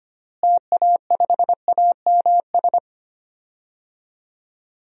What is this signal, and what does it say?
Morse code, heard as a keyed tone, about 25 wpm, 700 Hz. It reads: TA5AMS
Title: DXCC-100 Call Signs - 6 Characters - Repeated 3x - 25wpm